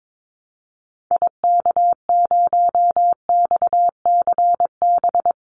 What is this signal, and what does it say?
Morse code, heard as a keyed tone, about 22 wpm, 700 Hz. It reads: IK0XCB
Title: Call Signs - 22wpm